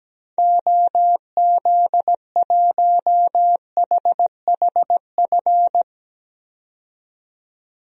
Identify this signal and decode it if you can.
Morse code, heard as a keyed tone, about 17 wpm, 700 Hz. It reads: OZ1HHF
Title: DXCC-100 Call Signs - 6 Characters - 17wpm